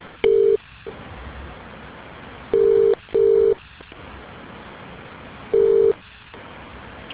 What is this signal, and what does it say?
Ringback tone (UK), 400 + 450 Hz.